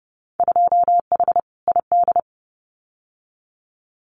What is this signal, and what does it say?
Morse code, heard as a keyed tone, about 30 wpm, 700 Hz. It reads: 2H ID
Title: Winter Field Day Exchanges 30wpm